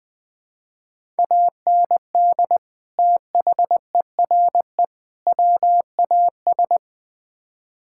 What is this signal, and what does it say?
Morse code, heard as a keyed tone, about 21 wpm, 700 Hz. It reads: AND THERE WAS
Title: Build A Phrase - 3-Word Phrases - 20wpm